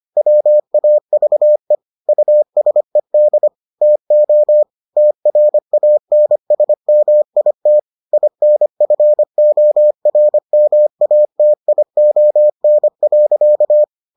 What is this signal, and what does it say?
Morse code, heard as a keyed tone, about 25 wpm, 600 Hz. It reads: WAVE USED TO TRANSMIT INFORMATION.